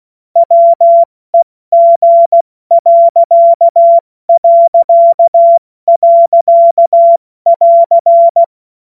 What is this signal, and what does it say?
Morse code, heard as a keyed tone, about 16 wpm, spaced as for 14 wpm, 680 Hz. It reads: WEG...<AR>
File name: Morsekurs / Witze / Witz 02 Januar 2024 70 F80